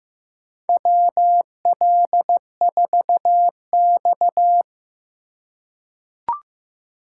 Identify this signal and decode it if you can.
Morse code, heard as a keyed tone, about 15 wpm, 700 Hz. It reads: WL4X
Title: USA Call Signs - 4 Characters - 3x Character Spacing - 15wpm